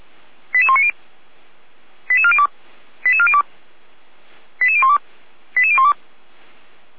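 Five-tone selective call (ZVEI), heard as quick runs of five tones, tones of about 80 ms. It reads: 88190 88402 88402 88012 88012
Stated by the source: Five-tone ZVEI selective call received by SDR, from audio.f85360.feuerwehr-orlamuende.org, a fire brigade (Feuerwehr Orlamünde) server.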